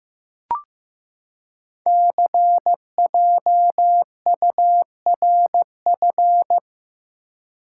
Morse code, heard as a keyed tone, about 15 wpm, 700 Hz. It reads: CJURF
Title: Code Groups - 5 Letters - 15wpm